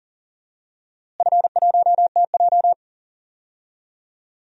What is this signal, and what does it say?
Morse code, heard as a keyed tone, about 40 wpm, 700 Hz. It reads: F1TJ
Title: DXCC-100 Call Signs - 4 Characters - Repeated 3x - 40wpm